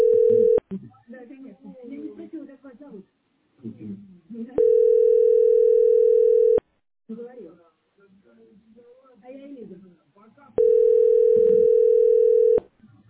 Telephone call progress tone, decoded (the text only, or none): ringback tone